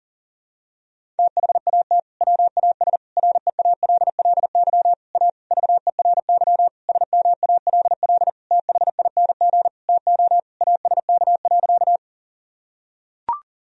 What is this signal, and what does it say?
Morse code, heard as a keyed tone, about 40 wpm, 700 Hz. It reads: THAT WAS REALLY A VERY SMALL THING TO ASK.
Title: Sentences from Top 1000 Words - Review Part 1 - 40wpm